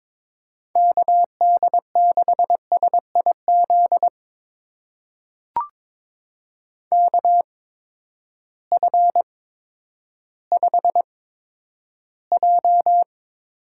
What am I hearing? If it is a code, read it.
Morse code, heard as a keyed tone, about 22 wpm, spaced as for 11 wpm, 700 Hz. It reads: KD6SIZ KF5J
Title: USA Call Signs - 6 Characters - 8x Character Spacing - 22wpm